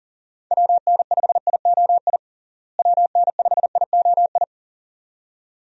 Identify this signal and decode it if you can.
Morse code, heard as a keyed tone, about 37 wpm, 700 Hz. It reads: WN5IOI WN5IOI
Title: USA Call Signs - 6 Characters - Repeated 2x - 40wpm